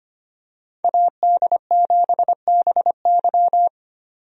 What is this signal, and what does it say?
Morse code, heard as a keyed tone, about 25 wpm, 700 Hz. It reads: AD7BY
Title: USA Call Signs - 5 Characters - 25wpm